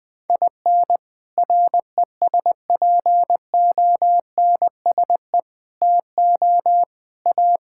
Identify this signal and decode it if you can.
Morse code, heard as a keyed tone, about 19 wpm, 700 Hz. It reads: IN RESPONSE TO A
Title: Sets of 6 Words - 20wpm